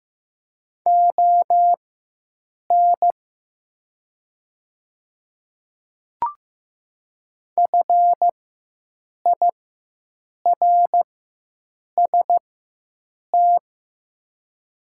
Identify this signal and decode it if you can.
Morse code, heard as a keyed tone, about 15 wpm, spaced as for 7 wpm, 700 Hz. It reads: ON FIRST
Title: 4x Character Spacing - Top 100 Words - 15wpm